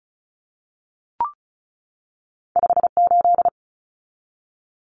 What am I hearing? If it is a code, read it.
Morse code, heard as a keyed tone, about 34 wpm, 700 Hz. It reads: 58
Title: Numbers - 2-Digit 35wpm